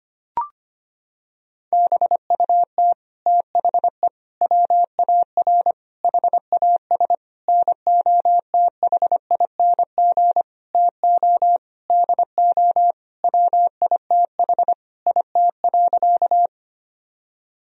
Morse code, heard as a keyed tone, about 25 wpm, 700 Hz. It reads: BUT THE WAR HAS NOTHING TO DO WITH IT.
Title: Sentences from Top 1000 Words - Review Part 2 - 25wpm